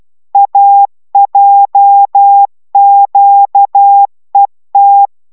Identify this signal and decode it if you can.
Morse code, heard as a keyed tone, about 12 wpm, 800 Hz. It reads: AJQET